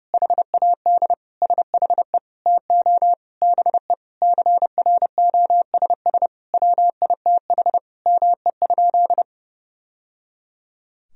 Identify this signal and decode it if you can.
Morse code, heard as a keyed tone, about 32 wpm, 700 Hz. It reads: HAD SHE TO BE CROSS WITH ME?
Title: Sentences from Top 1000 Words - Review Part 1 - 30wpm